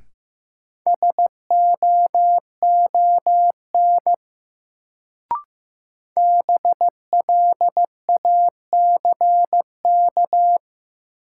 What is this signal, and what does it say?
Morse code, heard as a keyed tone, about 15 wpm, 700 Hz. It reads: SOON BLACK
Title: Sentences from Top 400 Words - New Word Drill - 15wpm